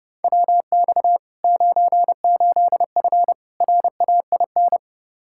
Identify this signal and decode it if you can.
Morse code, heard as a keyed tone, about 30 wpm, 700 Hz. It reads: WX 98F RAIN
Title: Practice QSO - Exchange 2 - Info - 30wpm